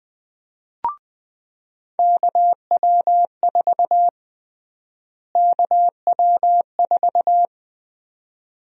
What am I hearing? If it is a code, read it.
Morse code, heard as a keyed tone, about 20 wpm, 700 Hz. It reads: KW4 KW4
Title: USA Call Sign Prefixes - Repeated 2x - 20wpm